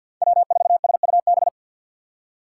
Morse code, heard as a keyed tone, about 47 wpm, 700 Hz. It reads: W4SUB
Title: USA Call Signs - 5 Characters - 2x Character Spacing - 50wpm